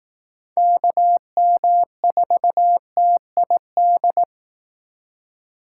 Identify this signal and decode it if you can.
Morse code, heard as a keyed tone, about 18 wpm, 700 Hz. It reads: KM4TID
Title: Speed Racing - USA 2x3 Call Signs 15wpm